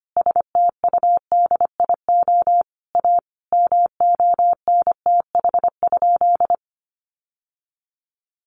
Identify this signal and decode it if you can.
Morse code, heard as a keyed tone, about 25 wpm, 700 Hz. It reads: STUDIO A MONTH?
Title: Sentences from Top 1200 Words 25wpm